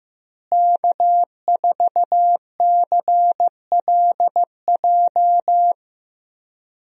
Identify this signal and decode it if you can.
Morse code, heard as a keyed tone, about 15 wpm, 700 Hz. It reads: K4CLJ
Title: USA Call Signs - 5 Characters - Repeated 3x - 15wpm